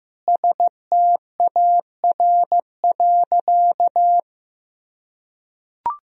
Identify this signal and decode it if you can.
Morse code, heard as a keyed tone, about 15 wpm, 700 Hz. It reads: STAR.
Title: Sentences from Top 1000 Words - Review Part 3 - 15wpm